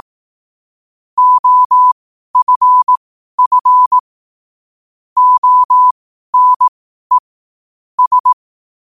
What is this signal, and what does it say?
Morse code, heard as a keyed tone, about 18 wpm, spaced as for 12 wpm, 1000 Hz. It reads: OFF ONE S
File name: qotdfiles / 12wpm